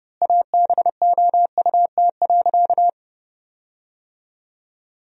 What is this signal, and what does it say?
Morse code, heard as a keyed tone, about 30 wpm, 700 Hz. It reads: ABOUT.
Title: 3x Word-Spacing - Sentences from Top 400 Words - 30wpm